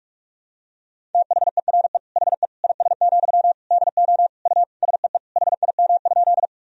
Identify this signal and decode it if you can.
Morse code, heard as a keyed tone, about 46 wpm, 700 Hz. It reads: THERE HE IS, DO U SEE HIM?
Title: Sentences from Top 100 Words and Abbreviations 45wpm